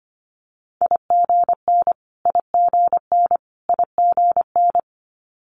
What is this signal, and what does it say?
Morse code, heard as a keyed tone, about 25 wpm, 700 Hz. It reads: IGN IGN IGN
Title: Multi-letter Phonemes Repeated 3x 25wpm